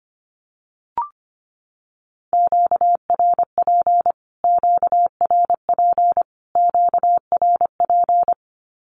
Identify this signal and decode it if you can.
Morse code, heard as a keyed tone, about 25 wpm, 700 Hz. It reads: QRP QRP QRP
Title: Q-Codes Repeated 3x 25wpm